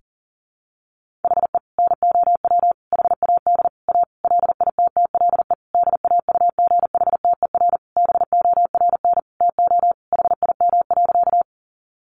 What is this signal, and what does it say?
Morse code, heard as a keyed tone, about 40 wpm, 700 Hz. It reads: HE NOW HAD A LITTLE DAUGHTER BORN TO HIM.
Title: Sentences from Top 1500 Words 40wpm